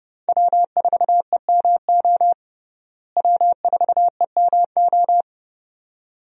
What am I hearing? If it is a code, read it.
Morse code, heard as a keyed tone, about 29 wpm, 700 Hz. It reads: W4EMO W4EMO
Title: USA Call Signs - 5 Characters - Repeated 2x - 30wpm